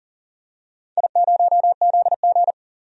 Morse code, heard as a keyed tone, about 40 wpm, 700 Hz. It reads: I0ZG
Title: DXCC-100 Call Signs - 4 Characters - 40wpm